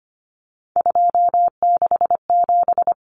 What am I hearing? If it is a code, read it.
Morse code, heard as a keyed tone, about 25 wpm, 690 Hz. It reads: 267
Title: Numbers - 3-Digit 25wpm